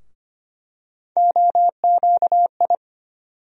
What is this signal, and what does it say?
Morse code, heard as a keyed tone, about 25 wpm, 700 Hz. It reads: OQI
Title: USA Call Sign Postfixes - 3 Letters 25wpm